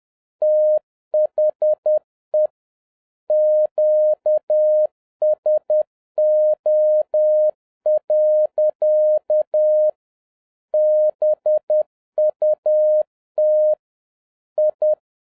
Morse code, heard as a keyed tone, about 10 wpm, 610 Hz. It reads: THE QSO. BUT I